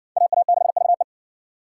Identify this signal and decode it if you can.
Morse code, heard as a keyed tone, about 53 wpm, 700 Hz. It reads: AA6VE